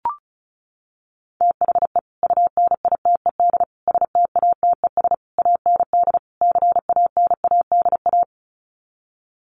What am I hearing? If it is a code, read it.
Morse code, heard as a keyed tone, about 34 wpm, 700 Hz. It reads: THE UNITED STATES AND CANADA
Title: Sets of 5 Words 35wpm